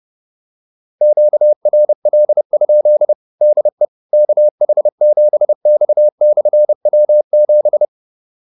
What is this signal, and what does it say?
Morse code, heard as a keyed tone, about 30 wpm, 600 Hz. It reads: QRL? DE KH7X/W7